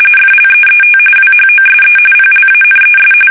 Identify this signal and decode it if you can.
RTTY, about 45.45 baud. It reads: FLEET WHOSE PLANES
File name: fsk850 / battleshipnj